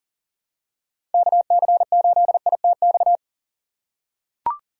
Morse code, heard as a keyed tone, about 44 wpm, 700 Hz. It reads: KC8ITX
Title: USA Call Signs - 6 Characters - 6x Character Spacing - 40wpm